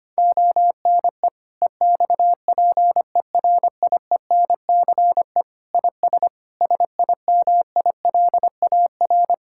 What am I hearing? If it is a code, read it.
Morse code, heard as a keyed tone, about 25 wpm, 700 Hz. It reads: ONE EXPERIENCE IS SIMILAR
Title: Sentences from Top 1000 Words - Review Part 5 - 25wpm